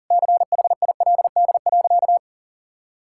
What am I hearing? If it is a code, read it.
Morse code, heard as a keyed tone, about 40 wpm, 700 Hz. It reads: CHILD.
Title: Sentences from Top 1600 Words 40wpm